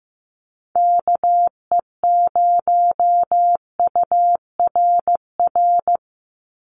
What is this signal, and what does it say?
Morse code, heard as a keyed tone, about 15 wpm, 700 Hz. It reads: KE0URR
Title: USA Call Signs 15wpm